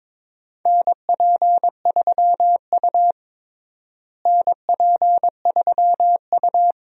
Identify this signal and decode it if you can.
Morse code, heard as a keyed tone, about 22 wpm, 700 Hz. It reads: NP3U NP3U
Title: USA Call Signs - 4 Characters - Repeated 2x - 22wpm